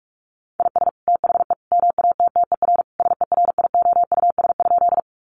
Morse code, heard as a keyed tone, about 44 wpm, 700 Hz. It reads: IS THE MATTER SERIOUS?
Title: Sentences from Top 1100 Words 45wpm